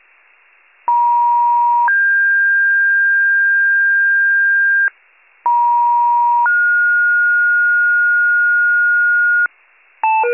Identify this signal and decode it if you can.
Two-tone page, heard as a one-second tone, then a three-second tone, about 950 then 1600 Hz, about 950 then 1430 Hz, with noise. It is heard from 0.9 to 9.5 s.